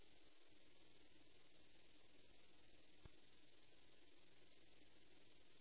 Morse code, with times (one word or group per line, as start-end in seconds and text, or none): none